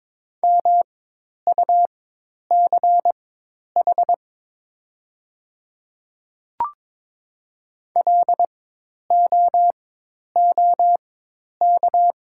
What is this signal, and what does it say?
Morse code, heard as a keyed tone, about 22 wpm, spaced as for 10 wpm, 700 Hz. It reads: MUCH LOOK
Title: 4x Character Spacing - Top 200 Words - 22wpm